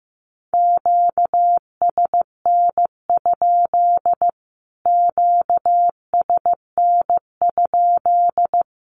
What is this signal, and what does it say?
Morse code, heard as a keyed tone, about 15 wpm, 700 Hz. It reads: QSN? QSN?
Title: Q-Codes Repeated 3x 15wpm